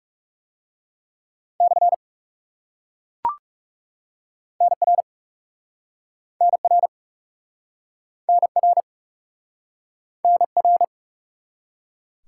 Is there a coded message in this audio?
Yes: Morse code from 1.6 to 10.8 s, heard as a keyed tone, about 40 wpm, 700 Hz. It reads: / NR NR NR NR